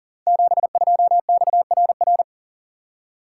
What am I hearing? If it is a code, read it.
Morse code, heard as a keyed tone, about 40 wpm, 700 Hz. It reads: 72XRR
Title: Code Groups - 5 Letter:Numbers - 40wpm